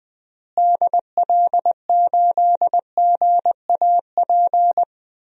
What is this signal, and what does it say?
Morse code, heard as a keyed tone, about 21 wpm, 700 Hz. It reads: DL8GAP
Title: DXCC-100 Call Signs - 6 Characters - Repeated 2x - 20wpm